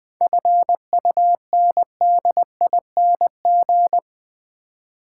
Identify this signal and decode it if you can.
Morse code, heard as a keyed tone, about 20 wpm, 700 Hz. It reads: FUNDING
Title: 7 Letter Words - First 3 Letters Given - 20wpm